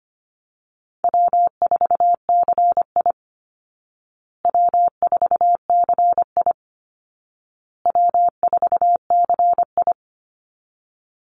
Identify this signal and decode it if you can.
Morse code, heard as a keyed tone, about 25 wpm, 700 Hz. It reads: W4CI W4CI W4CI
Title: Easy Call Signs Repeated 3x - 25wpm